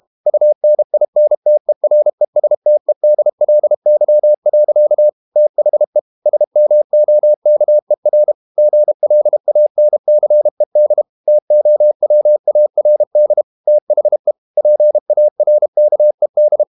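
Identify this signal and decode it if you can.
Morse code, heard as a keyed tone, about 32 wpm, 600 Hz. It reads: UNINTERESTEDLY. THE SMOKER GLANCED TOWARD THE PARKED
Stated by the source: Morse code practice at 32 wpm from CW med Gnister, reading Ken Follett's novel Never in English.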